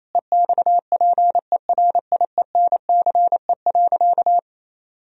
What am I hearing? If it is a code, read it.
Morse code, heard as a keyed tone, about 28 wpm, 700 Hz. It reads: EXPERIENCE.
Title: Sentences from Top 1000 Words - Review Part 3 - 28wpm